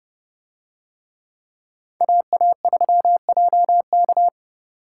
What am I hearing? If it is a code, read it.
Morse code, heard as a keyed tone, about 29 wpm, 700 Hz. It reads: AA3JK